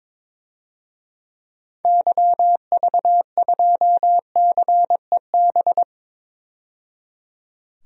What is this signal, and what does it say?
Morse code, heard as a keyed tone, about 22 wpm, 700 Hz. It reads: YV2CEB
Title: DXCC-100 Call Signs - 6 Characters - 22wpm